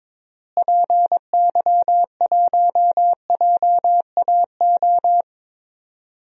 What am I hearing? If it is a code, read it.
Morse code, heard as a keyed tone, about 22 wpm, 700 Hz. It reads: PY1JAO
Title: DXCC-100 Call Signs - 6 Characters - Repeated 2x - 22wpm